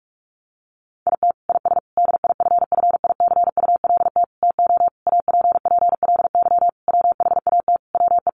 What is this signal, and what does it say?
Morse code, heard as a keyed tone, about 44 wpm, 700 Hz. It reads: IT IS DIFFICULT TO APPLY WHAT WE
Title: Sentences from Top 1200 Words 45wpm